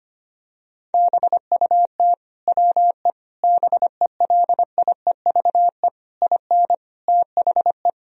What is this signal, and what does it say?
Morse code, heard as a keyed tone, about 25 wpm, 700 Hz. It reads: BUT WE BELIEVE IN THE
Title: Sentences from Top 1000 Words - Review Part 3 - 25wpm